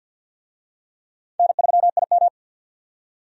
Morse code, heard as a keyed tone, about 49 wpm, 700 Hz. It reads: N3IM
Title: USA Call Signs - 4 Characters - 50wpm